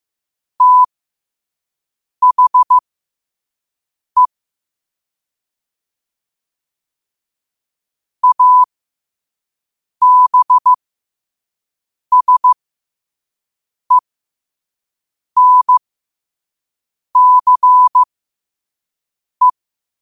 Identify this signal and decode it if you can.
Morse code, heard as a keyed tone, about 15 wpm, spaced as for 5 wpm, 1000 Hz. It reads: THE ABSENCE
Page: qotd-at-5-wpm